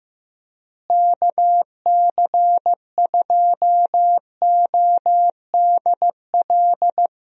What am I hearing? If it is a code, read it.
Morse code, heard as a keyed tone, about 15 wpm, 700 Hz. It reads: KC2ODL
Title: USA Call Signs - 6 Characters - Repeated 2x - 15wpm